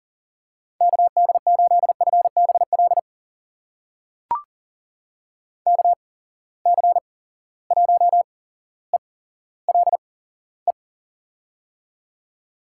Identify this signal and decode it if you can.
Morse code, heard as a keyed tone, about 39 wpm, spaced as for 17 wpm, 700 Hz. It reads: KD8FBL KC1ELE
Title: USA Call Signs - 6 Characters - 8x Character Spacing - 40wpm